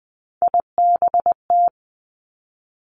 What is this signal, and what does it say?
Morse code, heard as a keyed tone, about 20 wpm, 700 Hz. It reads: IBT
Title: Three Challenging Letter-Number Combinations 20wpm